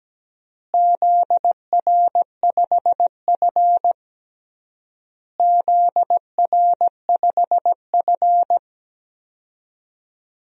Morse code, heard as a keyed tone, about 17 wpm, 700 Hz. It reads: ZR5F ZR5F
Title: DXCC-100 Call Signs - 4 Characters - Repeated 3x - 17wpm